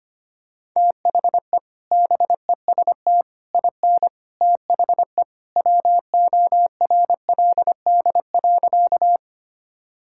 Morse code, heard as a keyed tone, about 25 wpm, 700 Hz. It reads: THE BEST IN THE WORLD.
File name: practice / 25wpm / Sentences with 3-Word Phrases - 25wpm